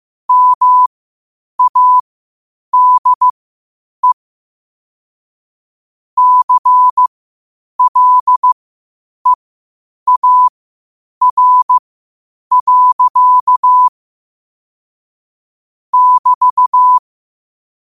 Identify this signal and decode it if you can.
Morse code, heard as a keyed tone, about 15 wpm, spaced as for 8 wpm, 1000 Hz. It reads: MADE CLEAR. <BT>
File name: qotdfiles / 8wpm